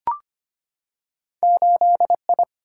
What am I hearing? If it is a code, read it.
Morse code, heard as a keyed tone, about 25 wpm, 700 Hz. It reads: 8 I
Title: Single-Letter Single-Number 25wpm